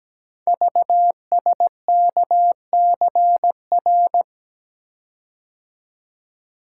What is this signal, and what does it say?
Morse code, heard as a keyed tone, about 17 wpm, 700 Hz. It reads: VSKCR
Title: Code Groups - 5 Letters - 17wpm